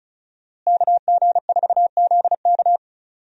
Morse code, heard as a keyed tone, about 36 wpm, 700 Hz. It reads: KG4ZK
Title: USA Call Signs - 5 Characters - Repeated 3x - 35wpm